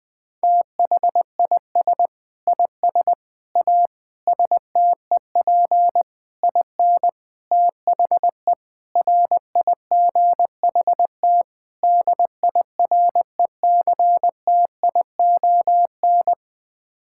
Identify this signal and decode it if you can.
Morse code, heard as a keyed tone, about 20 wpm, 700 Hz. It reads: THIS IS A STEP IN THE RIGHT DIRECTION
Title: Sets of 8 Words - 20wpm